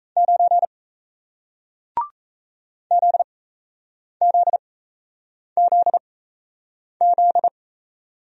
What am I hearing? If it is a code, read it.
Morse code, heard as a keyed tone, about 38 wpm, spaced as for 8 wpm, 700 Hz. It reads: 9 ZZZZ